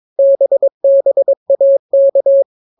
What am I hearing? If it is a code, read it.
Morse code, heard as a keyed tone, about 22 wpm, 550 Hz. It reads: BBAK